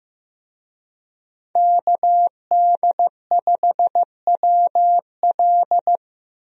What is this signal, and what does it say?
Morse code, heard as a keyed tone, about 15 wpm, 700 Hz. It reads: KD5WL
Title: USA Call Signs - 5 Characters - Repeated 2x - 15wpm